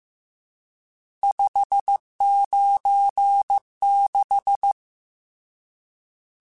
Morse code, heard as a keyed tone, about 15 wpm, 800 Hz. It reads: 596
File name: RSTs / RST-mp3s / Various RST @13